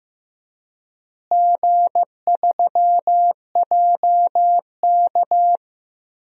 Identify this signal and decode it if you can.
Morse code, heard as a keyed tone, about 15 wpm, 700 Hz. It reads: G3JK
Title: DXCC-100 Call Signs - 4 Characters - Repeated 2x - 15wpm